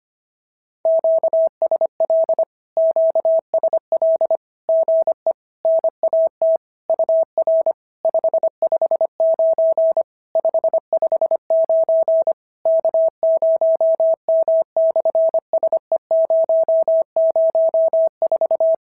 Morse code, heard as a keyed tone, about 24 wpm, 650 Hz. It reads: QSL QSL GE NAT UR 559 559 K0M/SE004
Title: SOTA QSO Practice 25wpm